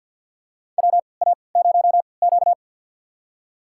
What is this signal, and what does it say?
Morse code, heard as a keyed tone, about 53 wpm, spaced as for 29 wpm, 700 Hz. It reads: WA0Q
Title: USA Call Signs - 3x Character Spacing - 50wpm